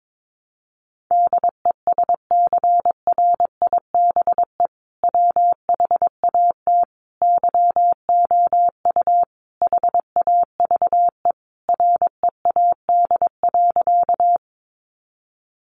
Morse code, heard as a keyed tone, about 22 wpm, 700 Hz. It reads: DESCRIBE WHAT YOU HAVE READ.
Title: Sentences from Top 1000 Words 22wpm